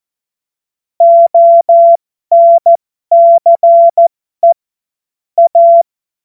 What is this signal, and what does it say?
Morse code, heard as a keyed tone, about 14 wpm, spaced as for 12 wpm, 680 Hz. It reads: ONCE A
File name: Morsekurs / Witze